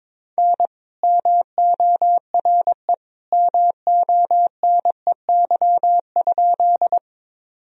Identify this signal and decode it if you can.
Morse code, heard as a keyed tone, about 22 wpm, 700 Hz. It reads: N MORE MONEY?
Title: Sentences from Top 400 Words and Abbreviations 22wpm